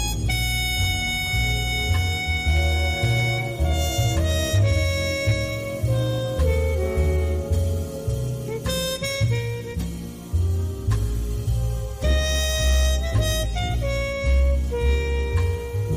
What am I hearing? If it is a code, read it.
Dial-up modem handshake.
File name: wrct-log hi- 23